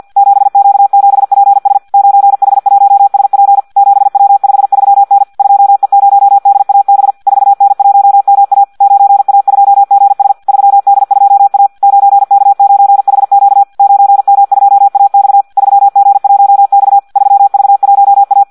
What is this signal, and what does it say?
Morse code, heard as a keyed tone, about 50 wpm, 800 Hz. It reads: 7Q7PS 9H1SP 7W53N 2E0DAB 4N1MA 9A2GS 3D2A 9K9HQ 9M2AX 3G1X 3V1A
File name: Audio practica CW / qrq-tx-50 2